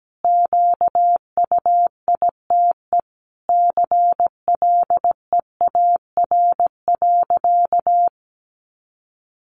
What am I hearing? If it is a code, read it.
Morse code, heard as a keyed tone, about 17 wpm, 700 Hz. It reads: QUITE CLEAR.
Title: Sentences from Top 1100 Words 17wpm